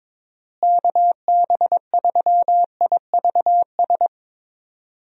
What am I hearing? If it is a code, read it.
Morse code, heard as a keyed tone, about 22 wpm, 700 Hz. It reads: KB3IVS
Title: USA Call Signs - 6 Characters - Repeated 2x - 22wpm